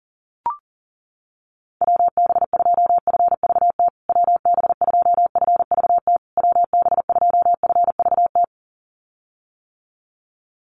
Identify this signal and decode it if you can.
Morse code, heard as a keyed tone, about 40 wpm, 700 Hz. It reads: WB2FVT WB2FVT WB2FVT
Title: USA Call Signs Repeated 3x 40wpm